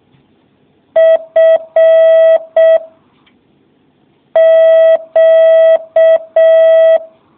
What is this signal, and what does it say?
Morse code, heard as a keyed tone, about 6 wpm, 650 Hz. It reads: F Q